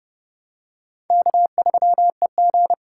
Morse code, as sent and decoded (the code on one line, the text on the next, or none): -.- ...-- . --.
K3EG